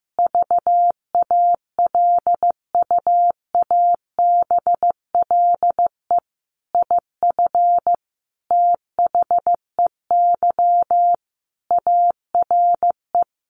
Morse code, heard as a keyed tone, about 15 wpm, 700 Hz. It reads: VALUABLE IF THEY ARE
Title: Sentences from Top 1400 Words 15wpm